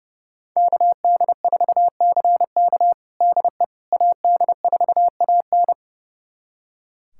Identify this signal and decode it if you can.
Morse code, heard as a keyed tone, about 30 wpm, 700 Hz. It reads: KD4CK DE AD4AN
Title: USA Callsign DE Callsign - 5 Characters - 30wpm